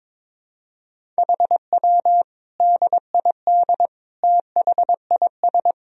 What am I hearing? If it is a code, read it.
Morse code, heard as a keyed tone, about 21 wpm, 700 Hz. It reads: HW DID THIS
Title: Sentences from Top 400 Words and Abbreviations 22wpm